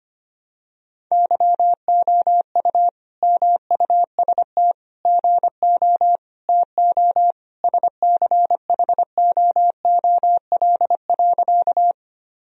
Morse code, heard as a keyed tone, about 25 wpm, 700 Hz. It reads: YOU MUST GO TO SCHOOL.
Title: Sentences from Top 1000 Words - Review Part 1 - 25wpm